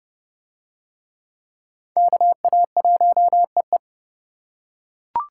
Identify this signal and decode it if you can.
Morse code, heard as a keyed tone, about 29 wpm, 700 Hz. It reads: KA1EE